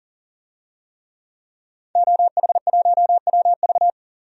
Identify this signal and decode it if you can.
Morse code, heard as a keyed tone, about 40 wpm, 700 Hz. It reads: OH1WV
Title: DXCC-100 Call Signs - 5 Characters - 40wpm